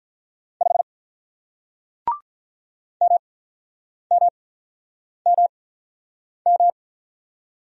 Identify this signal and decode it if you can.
Morse code, heard as a keyed tone, about 48 wpm, spaced as for 9 wpm, 700 Hz. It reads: 5 MMMM